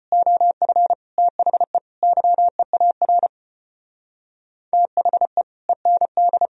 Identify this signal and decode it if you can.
Morse code, heard as a keyed tone, about 30 wpm, 700 Hz. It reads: OF THE YEAR THE END
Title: Speed-Racing - Sets of 5 Words 25wpm